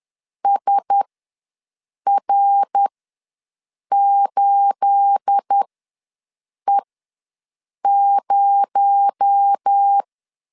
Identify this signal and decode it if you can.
Morse code, heard as a keyed tone, about 10 wpm, spaced as for 6 wpm, 800 Hz. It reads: SR8E0
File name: LEZIONI / 9 286 ws 10wpm 4032